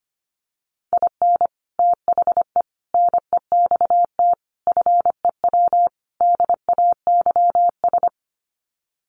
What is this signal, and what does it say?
Morse code, heard as a keyed tone, about 25 wpm, 700 Hz. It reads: IN THE NEXT FEW DAYS
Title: Sets of 5 Words Encore2 25wpm